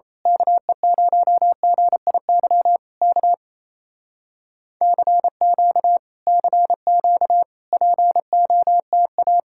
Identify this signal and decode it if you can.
Morse code, heard as a keyed tone, about 27 wpm, 700 Hz. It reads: KE0GIY K CQ CQ POTA
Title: Speed-Racing Common QSO Phrases - 28wpm